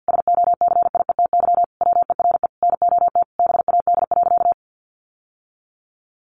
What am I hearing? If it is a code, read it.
Morse code, heard as a keyed tone, about 49 wpm, 700 Hz. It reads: SOCIETY WERE NOT BAD.
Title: Sentences from Top 900 Words 50wpm